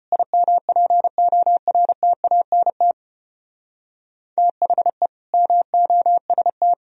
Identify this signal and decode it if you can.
Morse code, heard as a keyed tone, about 31 wpm, 700 Hz. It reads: IMPORTANT THE MOST
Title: Speed-Racing - Sets of 3 Words 25wpm